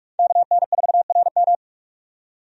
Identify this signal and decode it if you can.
Morse code, heard as a keyed tone, about 46 wpm, 700 Hz. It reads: KN4RM
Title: USA Call Signs - 5 Characters - 45wpm